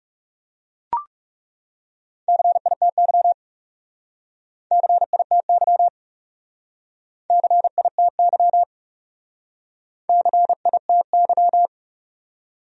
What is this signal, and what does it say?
Morse code, heard as a keyed tone, about 38 wpm, 700 Hz. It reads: CITY CITY CITY CITY